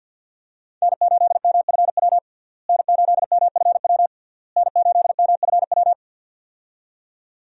Morse code, heard as a keyed tone, about 51 wpm, 700 Hz. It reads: N8MFW N8MFW N8MFW
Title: USA Call Signs - 5 Characters - Repeated 3x - 50wpm